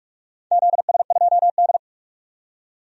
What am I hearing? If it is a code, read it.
Morse code, heard as a keyed tone, about 41 wpm, 700 Hz. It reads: ZS2D
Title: DXCC-100 Call Signs - 4 Characters - 45wpm